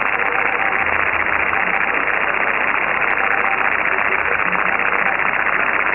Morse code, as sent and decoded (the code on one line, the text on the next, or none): none